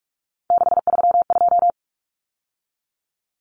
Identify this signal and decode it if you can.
Morse code, heard as a keyed tone, about 44 wpm, 700 Hz. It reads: 632
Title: Numbers - 3-Digit 45wpm